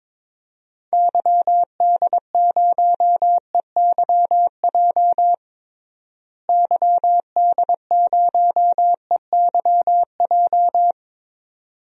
Morse code, heard as a keyed tone, about 22 wpm, 700 Hz. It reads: YD0EYJ YD0EYJ
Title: DXCC-100 Call Signs - 6 Characters - Repeated 2x - 22wpm